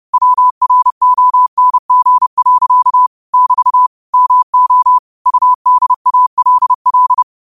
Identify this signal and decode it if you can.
Morse code, heard as a keyed tone, about 30 wpm, 1000 Hz. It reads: WRONG. <BT> MO UDALL
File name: qotdfiles / 30wpm